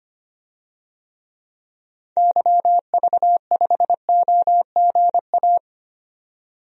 Morse code, heard as a keyed tone, about 25 wpm, 700 Hz. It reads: YV5OGA
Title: DXCC-100 Call Signs - 6 Characters - 25wpm